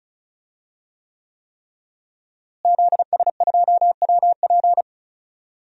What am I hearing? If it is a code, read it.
Morse code, heard as a keyed tone, about 36 wpm, 700 Hz. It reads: ZS2WP
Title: DXCC-100 Call Signs - 5 Characters - Repeated 3x - 35wpm